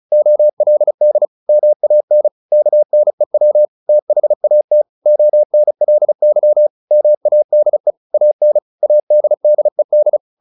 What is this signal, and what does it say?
Morse code, heard as a keyed tone, about 35 wpm, 600 Hz. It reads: OLD MAN KNEW THAT ONLY MADE AN ADDED
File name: the old man and the sea / the old man and the sea 1 35wpm